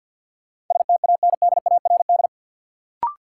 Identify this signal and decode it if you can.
Morse code, heard as a keyed tone, about 51 wpm, 700 Hz. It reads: STANDARD
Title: Sentences from Top 1000 Words - Review Part 6 - New Word Drill - 50wpm